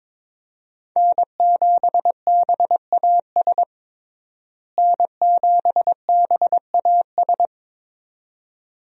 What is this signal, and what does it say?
Morse code, heard as a keyed tone, about 22 wpm, 700 Hz. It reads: N7BAS N7BAS
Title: USA Call Signs - 5 Characters - Repeated 2x - 22wpm